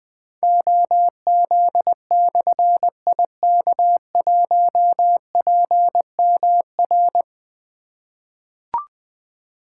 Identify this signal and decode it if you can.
Morse code, heard as a keyed tone, about 20 wpm, 700 Hz. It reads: OZ/IK1PMR
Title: Difficult Call Signs 20wpm